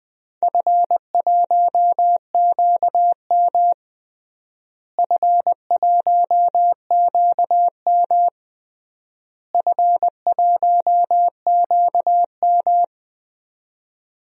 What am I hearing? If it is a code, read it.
Morse code, heard as a keyed tone, about 21 wpm, 700 Hz. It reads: F1QM F1QM F1QM
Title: DXCC-100 Call Signs - 4 Characters - Repeated 3x - 20wpm